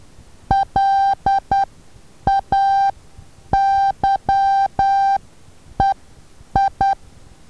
Morse code, heard as a keyed tone, about 10 wpm, spaced as for 8 wpm, 790 Hz. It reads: LAYEI
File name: Taviro HA6VA B